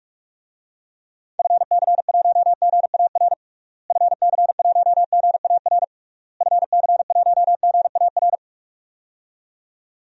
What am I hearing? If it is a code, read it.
Morse code, heard as a keyed tone, about 41 wpm, 700 Hz. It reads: FC1GAR FC1GAR FC1GAR
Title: DXCC-100 Call Signs - 6 Characters - Repeated 3x - 45wpm